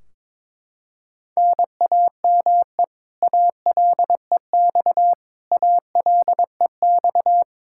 Morse code, heard as a keyed tone, about 21 wpm, 700 Hz. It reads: NAME ALEX ALEX
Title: Practice QSO - Exchange 1 - Beginning - 22wpm